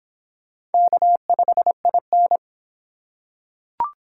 Morse code, heard as a keyed tone, about 24 wpm, 700 Hz. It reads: K5IN